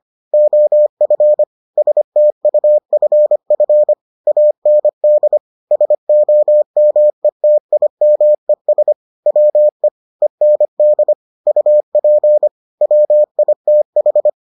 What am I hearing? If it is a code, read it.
Morse code, heard as a keyed tone, about 25 wpm, 600 Hz. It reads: OF STUFF AND SOMETIMES WE END UP WITH